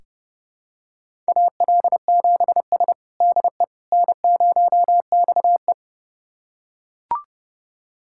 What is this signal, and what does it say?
Morse code, heard as a keyed tone, about 32 wpm, 700 Hz. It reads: AL7S DE N0XE
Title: USA Callsign DE Callsign - 4 Characters - 30wpm